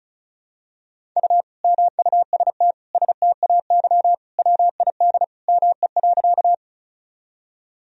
Morse code, heard as a keyed tone, about 36 wpm, 700 Hz. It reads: U MUST STAY WID ME.
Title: Sentences from Top 500 Words and Abbreviations 35wpm